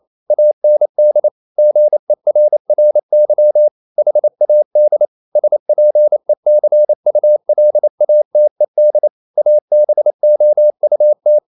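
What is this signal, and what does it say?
Morse code, heard as a keyed tone, about 28 wpm, 600 Hz. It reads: AND GERRY HAD SPECULATED ABOUT
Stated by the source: Morse code practice at 28 wpm, text of Ken Follett's novel Never in English, from CW med Gnister.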